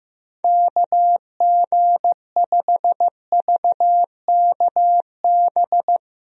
Morse code, heard as a keyed tone, about 15 wpm, 700 Hz. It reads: KG5VKB
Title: USA Call Signs - 6 Characters - Repeated 3x - 15wpm